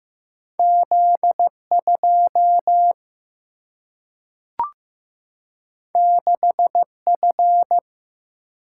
Morse code, heard as a keyed tone, about 15 wpm, 700 Hz. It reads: Z2 6F
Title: Single-Letter Single-Number 15wpm